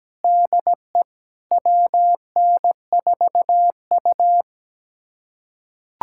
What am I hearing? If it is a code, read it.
Morse code, heard as a keyed tone, about 17 wpm, 700 Hz. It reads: DE WN4U
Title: USA Callsign DE Callsign - 4 Characters - 17wpm